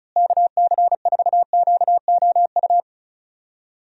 Morse code, heard as a keyed tone, about 34 wpm, 700 Hz. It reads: KC4QOU
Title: USA Call Signs - 6 Characters - Repeated 2x - 35wpm